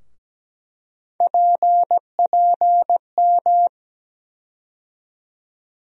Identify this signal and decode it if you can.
Morse code, heard as a keyed tone, about 17 wpm, 700 Hz. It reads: PPM
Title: USA Call Sign Postfixes - 3 Letters 17wpm